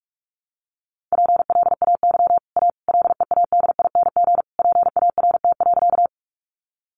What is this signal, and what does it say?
Morse code, heard as a keyed tone, about 44 wpm, 700 Hz. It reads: PLAY A LEADING PART.
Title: Sentences from Top 1100 Words 45wpm